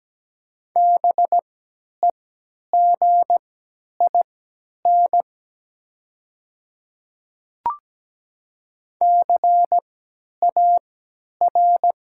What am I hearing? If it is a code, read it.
Morse code, heard as a keyed tone, about 17 wpm, spaced as for 10 wpm, 700 Hz. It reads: BEGIN CAR
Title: 3x Character Spacing - Top 300 Words - 17wpm